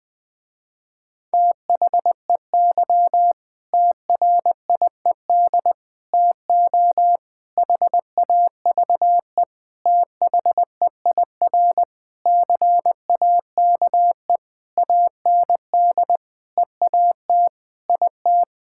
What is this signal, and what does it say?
Morse code, heard as a keyed tone, about 20 wpm, 700 Hz. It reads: THEY TRIED TO HAVE THEIR CAKE AND EAT IT
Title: Sentences with 8-Word Phrases - Spoken Beginning - 20wpm